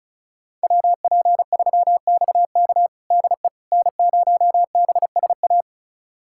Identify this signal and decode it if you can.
Morse code, heard as a keyed tone, about 36 wpm, 700 Hz. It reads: WP3XK DE N0BSA
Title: USA Callsign DE Callsign - 5 Characters - 35wpm